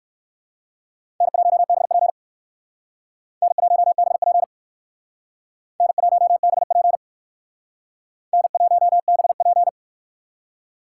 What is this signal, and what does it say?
Morse code, heard as a keyed tone, about 56 wpm, 700 Hz. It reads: N1BP N1BP N1BP N1BP
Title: Speed Racing - USA 1x2 Call Signs 45wpm